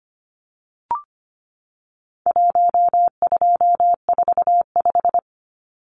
Morse code, heard as a keyed tone, about 25 wpm, 700 Hz. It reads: 1245
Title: Numbers - Contest Style - 4-Digit 25wpm